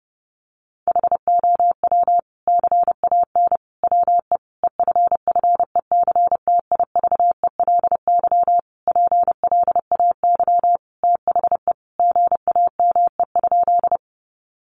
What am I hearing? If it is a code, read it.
Morse code, heard as a keyed tone, about 30 wpm, 700 Hz. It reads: HOW CAN WE EFFECTIVELY PLAY THE GAME?
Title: Sentences from Top 1400 Words 30wpm